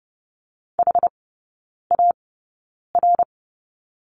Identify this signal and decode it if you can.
Morse code, heard as a keyed tone, about 30 wpm, spaced as for 9 wpm, 700 Hz. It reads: HAR
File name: Lesson 15e - Number 5 - Three Characters 30wpm